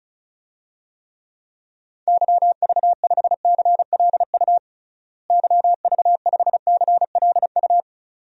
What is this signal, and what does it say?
Morse code, heard as a keyed tone, about 36 wpm, 700 Hz. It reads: YV5CLU YV5CLU
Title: DXCC-100 Call Signs - 6 Characters - Repeated 2x - 35wpm